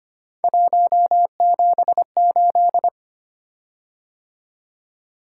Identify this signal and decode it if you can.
Morse code, heard as a keyed tone, about 25 wpm, 700 Hz. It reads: 178
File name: Three Letter-Number Combinations - 25wpm